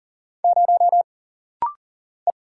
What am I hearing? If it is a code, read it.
Morse code, heard as a keyed tone, about 40 wpm, 700 Hz. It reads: 0 E
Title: Single Letter-Number - Rapid-Fire 40wpm